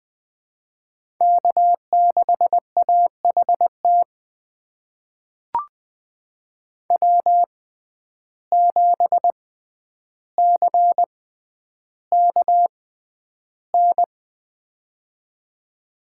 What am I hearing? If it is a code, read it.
Morse code, heard as a keyed tone, about 20 wpm, spaced as for 10 wpm, 700 Hz. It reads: K6AHT W7CKN
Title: USA Call Signs - 5 Characters - 6x Character Spacing - 20wpm